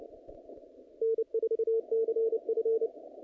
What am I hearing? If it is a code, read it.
Morse code, heard as a keyed tone, about 29 wpm, 450 Hz. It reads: N4CF
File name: contests / naqp-cw-k5tr-op-k5pi- -2300